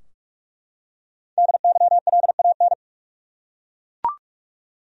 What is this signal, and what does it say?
Morse code, heard as a keyed tone, about 41 wpm, 700 Hz. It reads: DYLAN
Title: USA Common Names - 45wpm